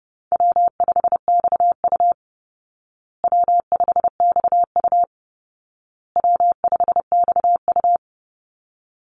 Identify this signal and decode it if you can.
Morse code, heard as a keyed tone, about 28 wpm, 700 Hz. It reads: W5XU W5XU W5XU
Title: Call Signs Repeated 2x - 30wpm